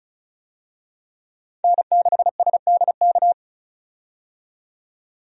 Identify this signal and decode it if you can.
Morse code, heard as a keyed tone, about 34 wpm, 700 Hz. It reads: N6SDK
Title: USA Call Signs - 5 Characters - 35wpm